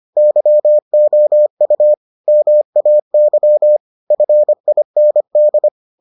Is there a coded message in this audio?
Yes: Morse code from 0.2 to 5.7 s, heard as a keyed tone, about 25 wpm, 600 Hz. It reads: YOU MAY FIND